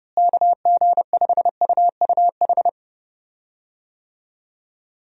Morse code, heard as a keyed tone, about 30 wpm, 700 Hz. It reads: KG5UUH
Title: USA Call Signs - 6 Characters - Repeated 3x - 30wpm